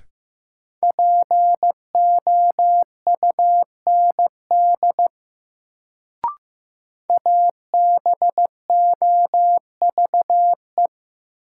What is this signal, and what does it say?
Morse code, heard as a keyed tone, about 15 wpm, 700 Hz. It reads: POUND ABOVE
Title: Sentences from Top 1000 Words - Review Part 2 - New Word Drill - 15wpm